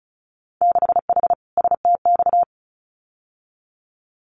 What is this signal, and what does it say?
Morse code, heard as a keyed tone, about 36 wpm, 700 Hz. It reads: 6H STX
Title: Winter Field Day Exchanges Encore 35wpm